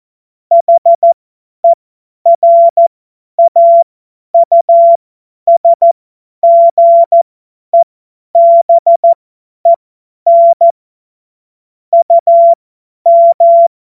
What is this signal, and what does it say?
Morse code, heard as a keyed tone, about 14 wpm, spaced as for 10 wpm, 680 Hz. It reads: HERAUSGEBEN UM
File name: Morsekurs / Witze / Witz 20. Januar 2026 50 F70